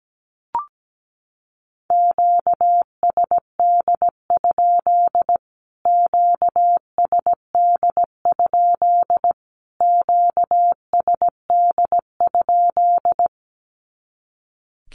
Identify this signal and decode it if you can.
Morse code, heard as a keyed tone, about 17 wpm, 700 Hz. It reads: QSD? QSD? QSD?
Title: Q-Codes Repeated 3x 17wpm